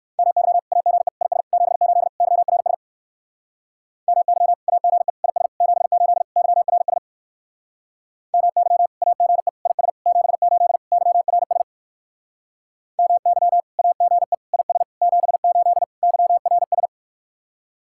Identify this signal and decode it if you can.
Morse code, heard as a keyed tone, about 56 wpm, 700 Hz. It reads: MY AGE IS 78 YRS MY AGE IS 78 YRS MY AGE IS 78 YRS MY AGE IS 78 YRS